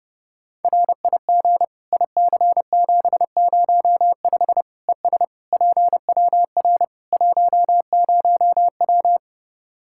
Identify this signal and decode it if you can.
Morse code, heard as a keyed tone, about 30 wpm, 700 Hz. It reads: RIG IC705 ES PWR 10W
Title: Practice QSO - Exchange 2 - Info - 30wpm